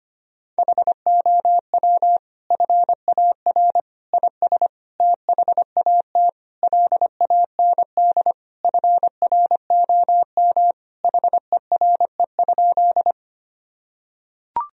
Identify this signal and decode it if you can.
Morse code, heard as a keyed tone, about 25 wpm, 700 Hz. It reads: HOW FAR IS THAT LAND FROM HERE?
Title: Sentences from Top 1000 Words - Review Part 1 - 25wpm